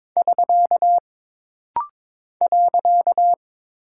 Morse code, heard as a keyed tone, about 23 wpm, 700 Hz. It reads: <SK> .
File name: practice / 22wpm / Single Punctuation-Prosign - Rapid-Fire - 22wpm